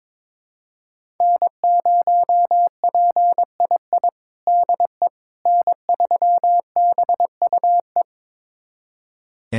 Morse code, heard as a keyed tone, about 21 wpm, 700 Hz. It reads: N0PII DE N3BUE
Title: USA Callsign DE Callsign - 5 Characters - 22wpm